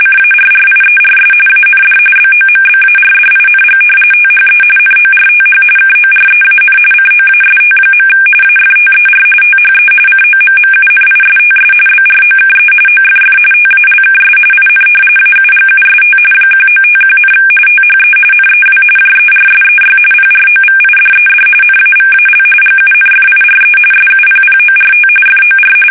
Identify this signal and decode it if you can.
RTTY, about 45.45 baud. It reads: CARRIERS AND BOMBARDING THE NORTH KOREAN COAST. AFTER THE COMMUNISTS BROKE OFF TALKS IN AUGUST 1951, THE NEW JERSEY PROVIDED NAVAL GUNFIRE SUPPORT FOR
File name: fsk850 / battleshipnj